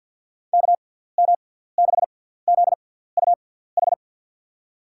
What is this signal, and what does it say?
Morse code, heard as a keyed tone, about 49 wpm, spaced as for 17 wpm, 700 Hz. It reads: KM6ZUH
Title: USA Call Signs - 6 Characters - 6x Character Spacing - 50wpm